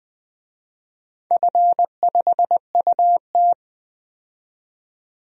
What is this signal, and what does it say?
Morse code, heard as a keyed tone, about 20 wpm, 700 Hz. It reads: F5UT
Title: DXCC-100 Call Signs - 4 Characters - 20wpm